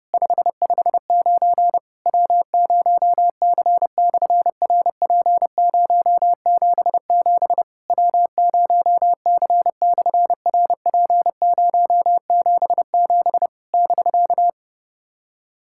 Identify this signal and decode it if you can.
Morse code, heard as a keyed tone, about 30 wpm, 700 Hz. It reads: 559 W0C/RP077 W0C/RP077 <BK>
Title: SOTA QSO Practice 30wpm